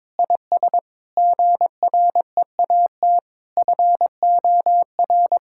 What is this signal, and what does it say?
Morse code, heard as a keyed tone, about 23 wpm, 700 Hz. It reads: IS GREAT FOR
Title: Sentences from Top 1000 Words - Review Part 5 - 22wpm